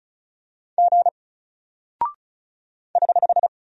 Morse code, heard as a keyed tone, about 36 wpm, 700 Hz. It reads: G <HH>
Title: Single Character (Letter:Number:Punctuation:Prosign) Alt 35wpm